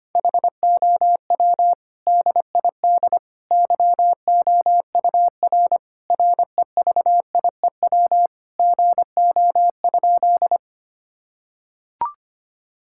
Morse code, heard as a keyed tone, about 25 wpm, 700 Hz. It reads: HOW DID YOUR REVIEW GO?
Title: Sentences from Top 1000 Words - Review Part 6 - 25wpm